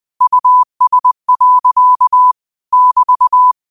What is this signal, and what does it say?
Morse code, heard as a keyed tone, about 20 wpm, 1000 Hz. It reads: US. <BT>
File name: qotdfiles / 20wpm